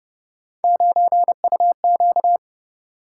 Morse code, heard as a keyed tone, about 30 wpm, 700 Hz. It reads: 9UQ